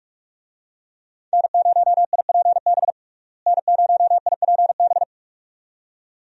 Morse code, heard as a keyed tone, about 46 wpm, 700 Hz. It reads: N0IPB N0IPB
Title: USA Call Signs - 5 Characters - Repeated 2x - 45wpm